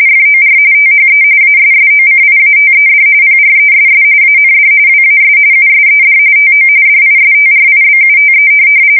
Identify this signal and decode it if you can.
RTTY, about 45.45 baud. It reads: FOR THE ASSOCIATED PRESS AND WAS USED TO DISTRIBUTE
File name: fsk170 / teletype-corp